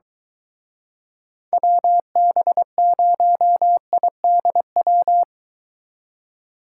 Morse code, heard as a keyed tone, about 24 wpm, 700 Hz. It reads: WB0IDW